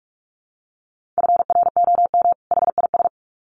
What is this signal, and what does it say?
Morse code, heard as a keyed tone, about 46 wpm, 700 Hz. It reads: FROM HIS
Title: Sets of 2 Words 45wpm